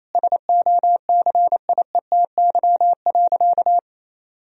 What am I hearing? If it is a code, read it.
Morse code, heard as a keyed tone, about 28 wpm, 700 Hz. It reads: SOCIETY.
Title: Sentences from Top 1000 Words - Review Part 4 - 28wpm